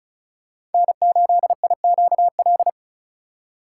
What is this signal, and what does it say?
Morse code, heard as a keyed tone, about 34 wpm, 700 Hz. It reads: N8IQL
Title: USA Call Signs - 5 Characters - 3x Character Spacing - 35wpm